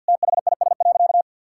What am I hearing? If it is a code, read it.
Morse code, heard as a keyed tone, about 50 wpm, 700 Hz. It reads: THIS.
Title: Sentences from Top 1000 Words - Review Part 6 - 50wpm